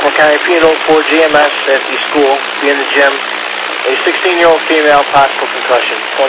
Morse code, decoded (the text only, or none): none